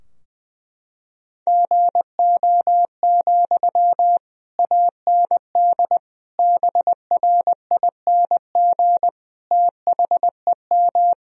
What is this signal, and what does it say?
Morse code, heard as a keyed tone, about 20 wpm, 700 Hz. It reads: GO, AND BRING THEM
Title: Sentences from Top 1000 Words - Review Part 3 - 20wpm